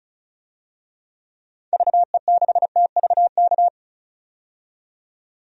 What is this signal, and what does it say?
Morse code, heard as a keyed tone, about 36 wpm, 700 Hz. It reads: VE6TVK
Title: DXCC-100 Call Signs - 6 Characters - 35wpm